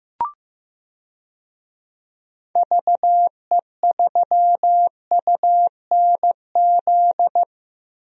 Morse code, heard as a keyed tone, about 15 wpm, 700 Hz. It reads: VE3UNZ
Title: DXCC-100 Call Signs - 6 Characters - Repeated 2x - 15wpm